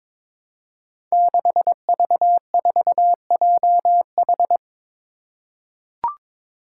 Morse code, heard as a keyed tone, about 22 wpm, 700 Hz. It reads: 6V4JH
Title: Five Challenging Letter-Number Combinations 22wpm